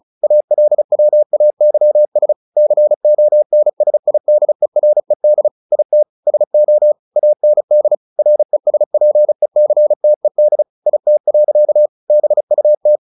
Morse code, heard as a keyed tone, about 34 wpm, 600 Hz. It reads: ALWAYS CONSIDERED IT SO AND RESPECTED IT. BUT
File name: the old man and the sea / the old man and the sea 1 35wpm